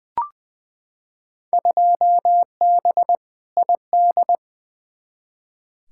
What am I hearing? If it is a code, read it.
Morse code, heard as a keyed tone, about 20 wpm, 700 Hz. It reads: 2B ID
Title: Field Day Exchanges Encore - 20wpm